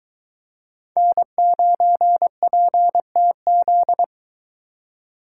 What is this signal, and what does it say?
Morse code, heard as a keyed tone, about 23 wpm, 700 Hz. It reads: N9PTZ